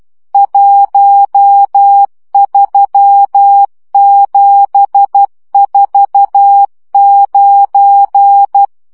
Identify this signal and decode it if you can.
Morse code, heard as a keyed tone, about 12 wpm, 800 Hz. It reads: 13749
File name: morze vizsga 2011-3 10